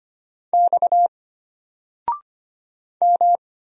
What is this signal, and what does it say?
Morse code, heard as a keyed tone, about 25 wpm, 700 Hz. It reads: X M
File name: Single-Letter-Number-25wpm